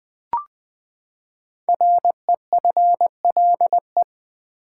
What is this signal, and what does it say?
Morse code, heard as a keyed tone, about 20 wpm, 700 Hz. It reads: REFLE
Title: Build A Word - 5 Letters At A Time - 10-Letter Words - 20wpm